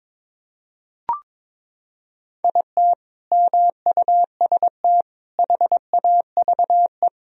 Morse code, heard as a keyed tone, about 21 wpm, 700 Hz. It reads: IT MUST HAVE
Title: Sentences from Top 1000 Words - Review Part 1 - 22wpm